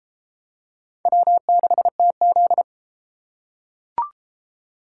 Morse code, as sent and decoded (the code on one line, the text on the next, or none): .-- -.... - --..
W6TZ